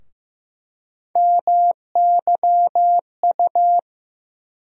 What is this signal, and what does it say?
Morse code, heard as a keyed tone, about 15 wpm, 700 Hz. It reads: MYU
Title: USA Call Sign Postfixes - 3 Letters 15wpm